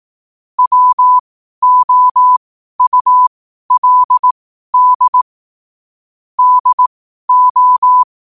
Morse code, heard as a keyed tone, about 18 wpm, spaced as for 12 wpm, 1000 Hz. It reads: WOULD DO
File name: qotdfiles / 12wpm